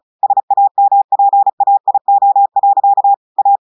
Morse code, heard as a keyed tone, about 36 wpm, 800 Hz. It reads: SAMPAIO. A
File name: audios / portugais / portugais texte en clair 1 vitesse 35 mots minutes